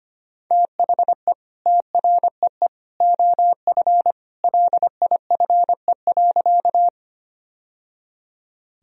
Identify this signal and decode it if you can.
Morse code, heard as a keyed tone, about 25 wpm, 700 Hz. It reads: THE TREE OF LIFE.
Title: Sentences from Top 1000 Words - Review Part 1 - 25wpm